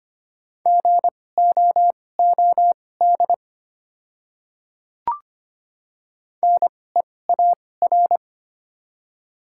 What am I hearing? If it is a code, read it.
Morse code, heard as a keyed tone, about 25 wpm, spaced as for 18 wpm, 690 Hz. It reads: GOOD NEAR
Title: 2x Character Spacing - Top 200 Words - 25wpm